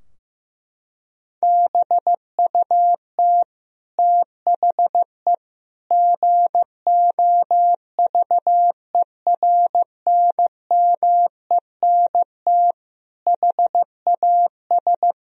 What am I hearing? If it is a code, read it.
Morse code, heard as a keyed tone, about 15 wpm, 700 Hz. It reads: BUT THE GOVERNMENT HAS
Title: Sentences from Top 1000 Words - Review Part 3 - 15wpm